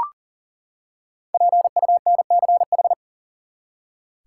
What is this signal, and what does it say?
Morse code, heard as a keyed tone, about 40 wpm, 700 Hz. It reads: PUNCH
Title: Five Letter Words - 40wpm